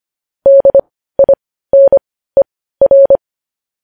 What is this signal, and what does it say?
Morse code, heard as a keyed tone, about 25 wpm, spaced as for 15 wpm, 550 Hz. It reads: DINER
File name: podcast / News headlines-1x-25x15